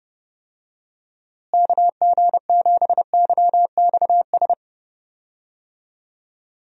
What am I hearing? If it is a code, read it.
Morse code, heard as a keyed tone, about 32 wpm, 700 Hz. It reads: KG7YXS
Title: USA Call Signs - 6 Characters - Repeated 3x - 30wpm